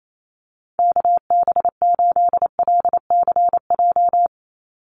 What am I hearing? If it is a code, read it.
Morse code, heard as a keyed tone, about 28 wpm, 700 Hz. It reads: KB8LCJ
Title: USA Call Signs 28wpm